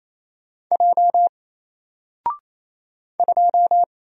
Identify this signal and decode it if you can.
Morse code, heard as a keyed tone, about 28 wpm, 700 Hz. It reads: J 2
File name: Single-Letter-Number-28wpm